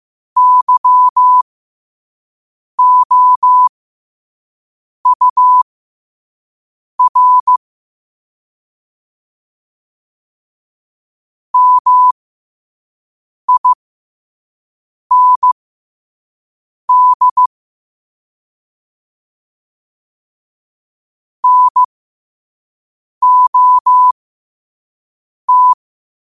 Morse code, heard as a keyed tone, about 15 wpm, spaced as for 5 wpm, 1000 Hz. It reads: YOUR MIND NOT
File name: qotdfiles / 5wpm